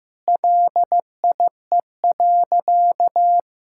Morse code, heard as a keyed tone, about 15 wpm, 700 Hz. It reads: LIE.
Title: Sentences from Top 1600 Words 15wpm